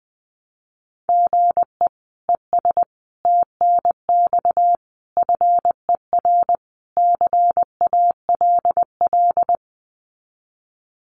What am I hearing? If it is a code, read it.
Morse code, heard as a keyed tone, about 20 wpm, 700 Hz. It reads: GE ES TNX FER CALL
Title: Common QSO Phrases Encore 20wpm